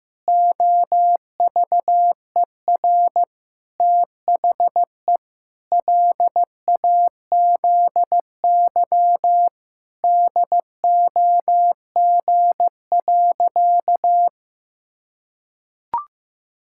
Morse code, heard as a keyed tone, about 15 wpm, 700 Hz. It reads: OVER THE LAZY DOG.
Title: Pangram Challenge v2 - 15wpm